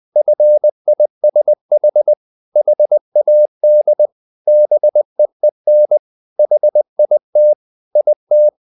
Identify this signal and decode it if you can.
Morse code, heard as a keyed tone, about 20 wpm, 600 Hz. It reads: FISH HAD BEEN HIT IT
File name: the old man and the sea part 2 20wpm